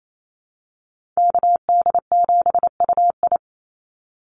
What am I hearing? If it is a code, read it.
Morse code, heard as a keyed tone, about 28 wpm, 700 Hz. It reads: KD7UI
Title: Call Signs Repeated 2x - 28wpm